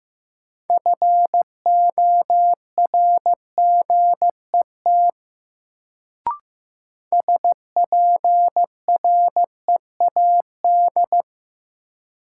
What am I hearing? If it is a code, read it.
Morse code, heard as a keyed tone, about 15 wpm, 700 Hz. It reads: FORGET SPREAD
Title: Sentences from Top 1200 Words - New Word Drill Alt - 15wpm